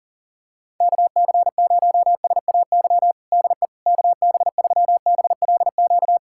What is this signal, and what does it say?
Morse code, heard as a keyed tone, about 40 wpm, 700 Hz. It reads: KC0SAY DE KB3BLQ
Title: USA Callsign DE Callsign - 6 Characters - 40wpm